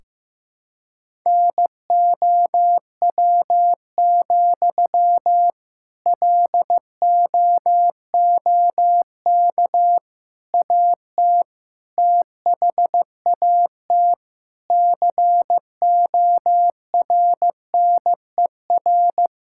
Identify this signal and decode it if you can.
Morse code, heard as a keyed tone, about 15 wpm, 700 Hz. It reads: NOW, LOOK AT THAT CORNER
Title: Sentences from Top 1600 Words 15wpm